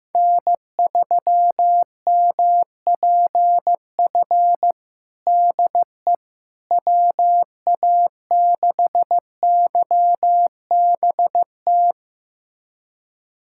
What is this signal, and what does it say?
Morse code, heard as a keyed tone, about 16 wpm, 700 Hz. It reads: N3MPF DE WA6YBT
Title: USA Callsign DE Callsign 15wpm